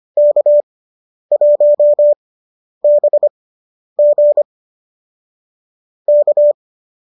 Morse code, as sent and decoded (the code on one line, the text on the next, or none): -.- .---- -... --. / -.-
K1BG K